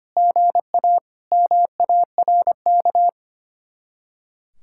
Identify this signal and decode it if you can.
Morse code, heard as a keyed tone, about 33 wpm, 700 Hz. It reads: GA MARK